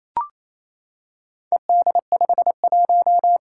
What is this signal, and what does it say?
Morse code, heard as a keyed tone, about 28 wpm, 700 Hz. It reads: ED51
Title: Four Challenging Letter-Number Combinations 28wpm